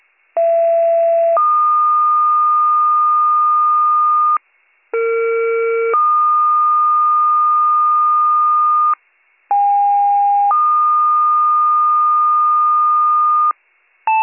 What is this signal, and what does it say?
Two-tone page, heard as a one-second tone, then a three-second tone, about 670 then 1220 Hz, about 460 then 1220 Hz, about 810 then 1220 Hz. It is heard from 0.4 to 13.5 s.